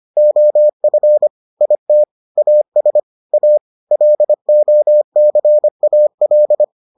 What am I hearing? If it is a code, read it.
Morse code, heard as a keyed tone, about 25 wpm, 600 Hz. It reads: OF IT AS A LOCAL